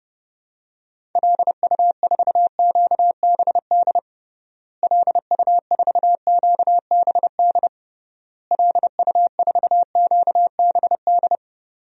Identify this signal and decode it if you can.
Morse code, heard as a keyed tone, about 30 wpm, 700 Hz. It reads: LU4QBD LU4QBD LU4QBD
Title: DXCC-100 Call Signs - 6 Characters - Repeated 3x - 30wpm